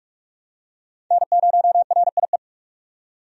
Morse code, heard as a keyed tone, about 46 wpm, 700 Hz. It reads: N0RIE
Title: USA Call Signs - 5 Characters - 45wpm